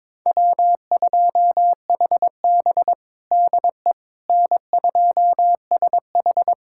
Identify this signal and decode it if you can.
Morse code, heard as a keyed tone, about 21 wpm, 700 Hz. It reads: W2HB DE N2SH
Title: USA Callsign DE Callsign - 4 Characters - 22wpm